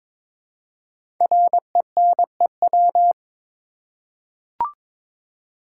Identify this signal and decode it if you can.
Morse code, heard as a keyed tone, about 22 wpm, 690 Hz. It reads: RENEW
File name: 5 Letter Words - First 3 Letters Given - 22wpm